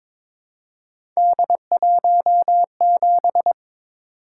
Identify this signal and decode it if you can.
Morse code, heard as a keyed tone, about 22 wpm, 700 Hz. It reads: D17
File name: practice / 22wpm / Three Letter-Number Combinations - 22wpm